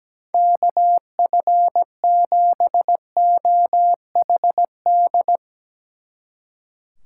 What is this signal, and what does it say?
Morse code, heard as a keyed tone, about 19 wpm, 700 Hz. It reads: KF7OHD
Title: Speed Racing - USA 2x3 Call Signs 17wpm